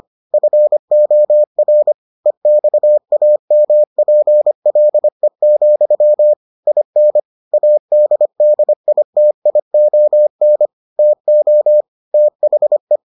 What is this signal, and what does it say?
Morse code, heard as a keyed tone, about 25 wpm, 600 Hz. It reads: FOR EXAMPLE, IN ADDITION TO THE